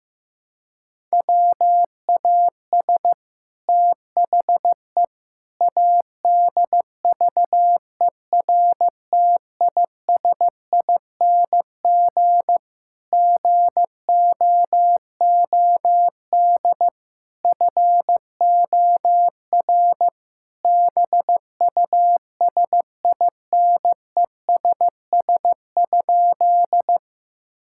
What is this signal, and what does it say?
Morse code, heard as a keyed tone, about 15 wpm, 700 Hz. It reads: WAS THE ADVERTISING GOOD FOR BUSINESS?
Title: Sentences from Top 1000 Words - Review Part 6 - 15wpm